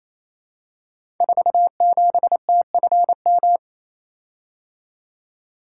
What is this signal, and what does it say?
Morse code, heard as a keyed tone, about 28 wpm, 700 Hz. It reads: 47TFM
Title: Code Groups - 5 Letter:Numbers - 28wpm